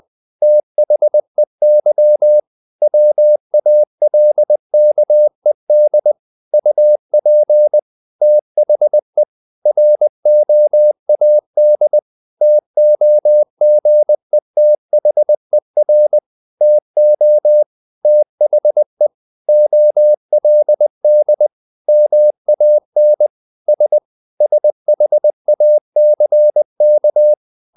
Morse code, heard as a keyed tone, about 20 wpm, 600 Hz. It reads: THEY WALKED UP THE ROAD TOGETHER TO THE OLD MAN S SHACK
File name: the old man and the sea / the old man and the sea part 1 20wpm